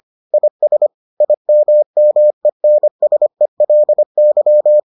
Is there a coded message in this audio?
Yes: Morse code from 0.3 to 4.8 s, heard as a keyed tone, about 25 wpm, 600 Hz. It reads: IS IMMENSELY